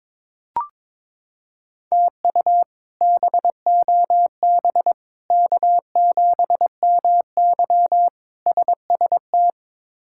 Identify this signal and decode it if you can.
Morse code, heard as a keyed tone, about 22 wpm, 700 Hz. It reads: TU BOB K7MY SST
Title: K1USN SST QSO Practice - 22wpm